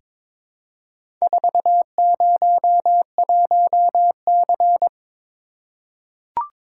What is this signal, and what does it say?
Morse code, heard as a keyed tone, about 21 wpm, 700 Hz. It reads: 401C
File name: practice / 22wpm / Four Letter-Number Combinations - 22wpm